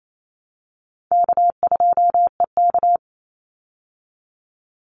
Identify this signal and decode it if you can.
Morse code, heard as a keyed tone, about 28 wpm, 700 Hz. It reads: K2EK
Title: Easy Call Signs Repeated 2x - 28wpm